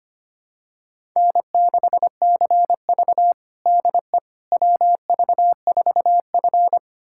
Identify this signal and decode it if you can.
Morse code, heard as a keyed tone, about 25 wpm, 700 Hz. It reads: N6CV DE WV4F
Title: USA Callsign DE Callsign - 4 Characters - 25wpm